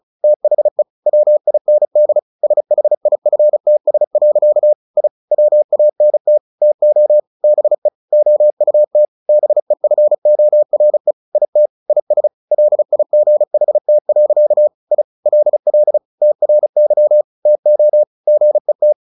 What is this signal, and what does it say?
Morse code, heard as a keyed tone, about 34 wpm, 600 Hz. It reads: THE WIND SHIFTS. I WANT TO BE OUT BEFORE IT IS LIGHT. I LL TRY TO GET
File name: the old man and the sea / the old man and the sea 1 35wpm